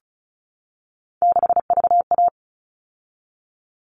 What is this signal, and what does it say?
Morse code, heard as a keyed tone, about 34 wpm, 700 Hz. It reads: 6VA
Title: All Cut Numbers - 3-Digit 35wpm